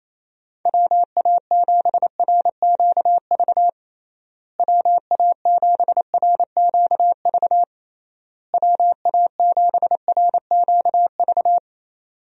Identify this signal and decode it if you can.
Morse code, heard as a keyed tone, about 28 wpm, 700 Hz. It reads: WA7RQV WA7RQV WA7RQV
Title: USA Call Signs - 6 Characters - Repeated 3x - 28wpm